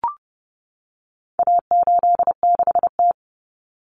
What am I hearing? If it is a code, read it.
Morse code, heard as a keyed tone, about 30 wpm, 700 Hz. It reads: A86T
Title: Common Cut Numbers - 4-Digit 30wpm